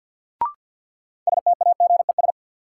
Morse code, heard as a keyed tone, about 49 wpm, 700 Hz. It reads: STAGES
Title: Sets of 5 Words Encore - New Word Drill - 50wpm